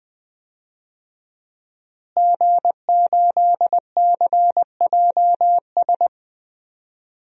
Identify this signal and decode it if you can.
Morse code, heard as a keyed tone, about 20 wpm, 700 Hz. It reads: G8CJS
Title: DXCC-100 Call Signs - 5 Characters - Repeated 3x - 20wpm